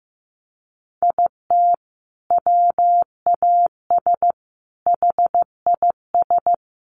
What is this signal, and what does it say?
Morse code, heard as a keyed tone, about 14 wpm, 700 Hz. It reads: IT WAS HIS
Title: Sentences from Top 1200 Words 15wpm